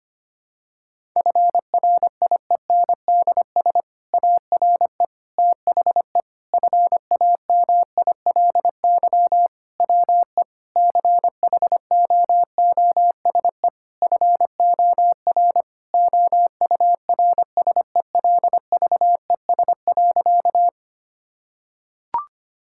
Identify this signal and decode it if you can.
Morse code, heard as a keyed tone, about 25 wpm, 700 Hz. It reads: FRIENDS ARE THE FAMILY WE CHOOSE FOR OURSELVES.
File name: practice / 25wpm / Difficult Sentences with Hints - 25wpm